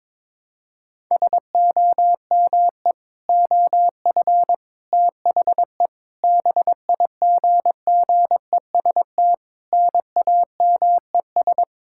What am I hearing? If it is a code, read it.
Morse code, heard as a keyed tone, about 22 wpm, 700 Hz. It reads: SOME OF THE BIGGEST NAMES
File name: Sentences with 7-Word Phrases - 22wpm